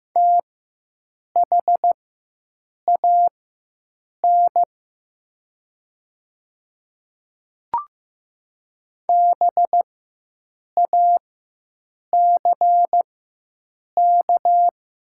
Morse code, heard as a keyed tone, about 15 wpm, spaced as for 7 wpm, 700 Hz. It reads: THAN BACK
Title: 4x Character Spacing - Top 200 Words - 15wpm